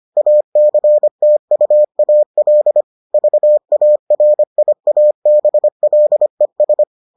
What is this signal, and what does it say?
Morse code, heard as a keyed tone, about 25 wpm, 600 Hz. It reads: ACTUAL VARIABLES